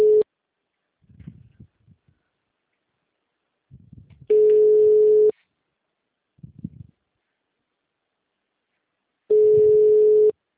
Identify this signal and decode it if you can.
Ringback tone (Europe, 425 Hz).
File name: out- -2011- - - .122